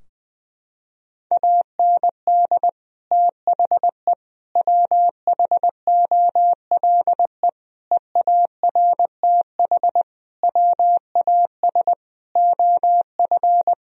Morse code, heard as a keyed tone, about 20 wpm, 700 Hz. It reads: AND THE WHOLE EARTH WAS OF
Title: Sentences from Top 1000 Words - Review Part 3 - 20wpm